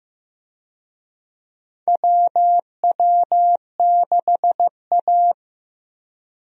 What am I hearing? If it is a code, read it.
Morse code, heard as a keyed tone, about 16 wpm, 700 Hz. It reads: WW6A
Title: USA Call Signs - 4 Characters - Repeated 2x - 15wpm